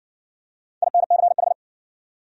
Morse code, heard as a keyed tone, about 54 wpm, 700 Hz. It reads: INCH